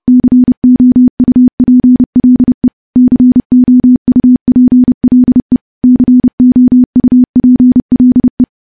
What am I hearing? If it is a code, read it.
Morse code, heard as a keyed tone, about 30 wpm, 250 Hz. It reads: COUPLE COUPLE COUPLE
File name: morse / morse-training-30WPM